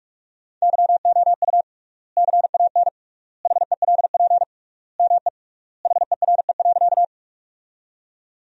Morse code, heard as a keyed tone, about 46 wpm, 700 Hz. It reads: YOU CAN HELP ME HERE.
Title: 3x Word-Spacing - Sentences from Top 200 Words - 45wpm